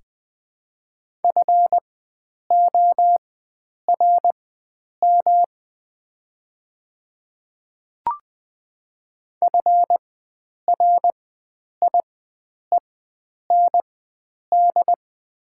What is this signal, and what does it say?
Morse code, heard as a keyed tone, about 20 wpm, spaced as for 9 wpm, 700 Hz. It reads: FORM FRIEND
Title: 4x Character Spacing - Top 300 Words - 20wpm